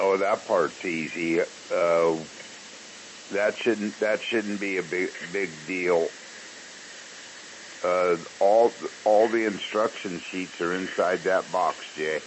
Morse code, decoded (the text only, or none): none